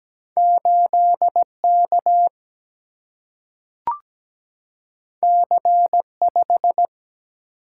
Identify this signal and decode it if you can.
Morse code, heard as a keyed tone, about 17 wpm, 700 Hz. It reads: 8K C5
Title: Single-Letter Single-Number 17wpm